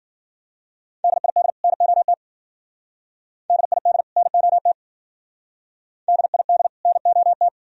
Morse code, heard as a keyed tone, about 50 wpm, 700 Hz. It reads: DID NOT DID NOT DID NOT